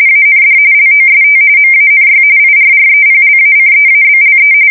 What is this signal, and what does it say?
RTTY, about 45.45 baud. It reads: AND LOST MANY IRREPLACEABLE
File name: fsk170 / battleshipnj